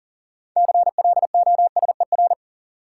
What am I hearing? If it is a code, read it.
Morse code, heard as a keyed tone, about 40 wpm, 700 Hz. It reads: CLOSER
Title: Sets of 5 Words Encore - New Word Drill - 40wpm